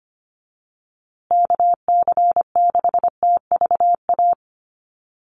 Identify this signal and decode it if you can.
Morse code, heard as a keyed tone, about 25 wpm, 700 Hz. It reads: KC6TVA
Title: USA Call Signs Repeated 3x 25wpm